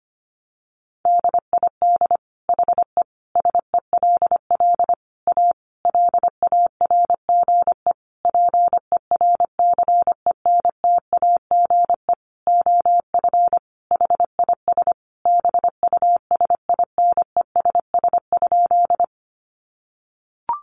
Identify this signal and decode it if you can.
Morse code, heard as a keyed tone, about 25 wpm, 700 Hz. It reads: DID HE SELL A LARGE PERCENTAGE OF HIS BUSINESS?
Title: Sentences from Top 1300 Words 25wpm